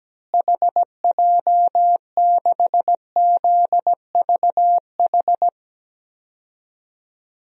Morse code, heard as a keyed tone, about 17 wpm, 700 Hz. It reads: HJ6ZVH
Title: DXCC-100 Call Signs - 6 Characters - Repeated 3x - 17wpm